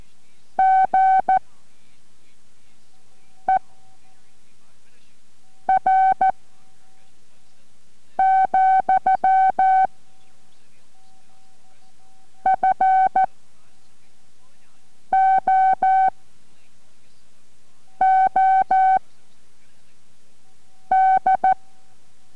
Morse code, heard as a keyed tone, about 14 wpm, 760 Hz. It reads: G E R , F O O D